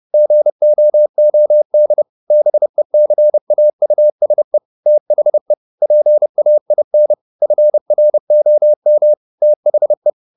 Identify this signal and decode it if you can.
Morse code, heard as a keyed tone, about 30 wpm, 600 Hz. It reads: GOOD BECAUSE THE PAIN FROM THE